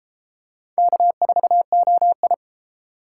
Morse code, heard as a keyed tone, about 29 wpm, 700 Hz. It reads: K4OI